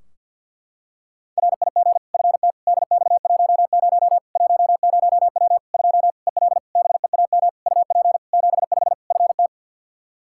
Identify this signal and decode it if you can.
Morse code, heard as a keyed tone, about 47 wpm, 700 Hz. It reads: RIG FT DX10 10W 2 EL BEAM UP 75 FT
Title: K1USN SST QSO Practice 50wpm